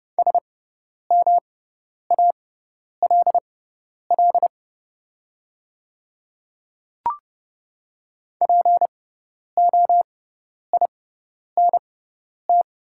Morse code, heard as a keyed tone, about 30 wpm, spaced as for 10 wpm, 700 Hz. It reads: SMALL POINT
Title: 6x Character Spacing - Top 200 Words - 30wpm